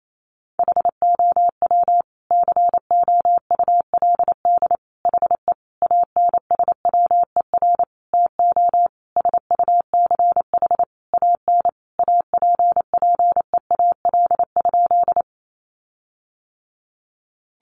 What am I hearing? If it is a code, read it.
Morse code, heard as a keyed tone, about 28 wpm, 700 Hz. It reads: HOW COULD HE ANSWER TO SUCH AN APPEAL?
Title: Sentences from Top 1300 Words 28wpm